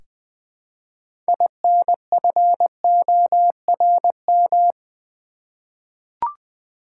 Morse code, heard as a keyed tone, about 20 wpm, 700 Hz. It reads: INFORM
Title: Six Letter Words - 20wpm